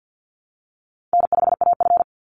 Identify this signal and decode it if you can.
Morse code, heard as a keyed tone, about 50 wpm, 700 Hz. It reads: N5AF
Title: Easy Call Signs Repeated 2x - 50wpm